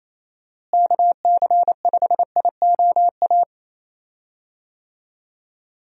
Morse code, heard as a keyed tone, about 28 wpm, 700 Hz. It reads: KC5IOA
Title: USA Call Signs - 6 Characters - Repeated 3x - 28wpm